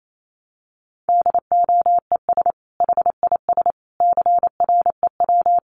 Morse code, heard as a keyed tone, about 28 wpm, 700 Hz. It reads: DOES HIS CREW
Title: Sentences from Top 1500 Words 28wpm